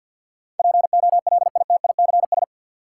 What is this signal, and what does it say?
Morse code, heard as a keyed tone, about 47 wpm, 700 Hz. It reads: POLITICS
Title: Sentences from Top 1000 Words - New Word Drill - 50wpm